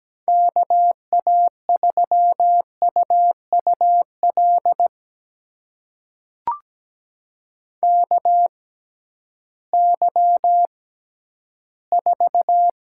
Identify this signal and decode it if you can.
Morse code, heard as a keyed tone, about 17 wpm, spaced as for 11 wpm, 700 Hz. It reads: KA3UUL KY4
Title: USA Call Signs - 6x Character Spacing - 17wpm